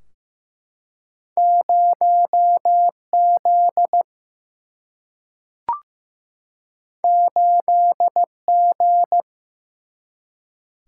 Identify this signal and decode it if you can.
Morse code, heard as a keyed tone, about 15 wpm, 700 Hz. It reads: 0Z 8G
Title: Single-Letter Single-Number 15wpm